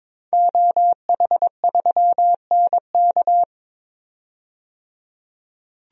Morse code, heard as a keyed tone, about 22 wpm, 700 Hz. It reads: OH3NK
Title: DXCC-100 Call Signs - 5 Characters - 22wpm